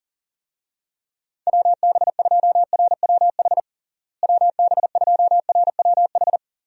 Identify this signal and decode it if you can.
Morse code, heard as a keyed tone, about 37 wpm, 700 Hz. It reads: WB2RWH WB2RWH
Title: USA Call Signs - 6 Characters - Repeated 3x - 40wpm